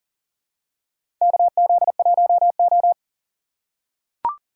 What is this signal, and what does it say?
Morse code, heard as a keyed tone, about 40 wpm, 700 Hz. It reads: KZ1O
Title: USA Call Signs - 4 Characters - 4x Character Spacing - 40wpm